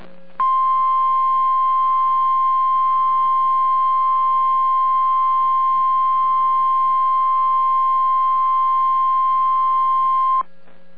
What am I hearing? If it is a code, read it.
Emergency Alert System (SAME) tones (the 1050 Hz attention tone). It is heard from 0.3 to 10.5 s.